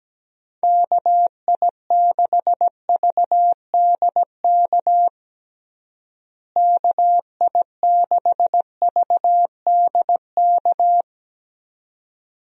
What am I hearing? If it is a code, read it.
Morse code, heard as a keyed tone, about 17 wpm, 700 Hz. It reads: KI6VDK KI6VDK
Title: USA Call Signs - 6 Characters - Repeated 2x - 17wpm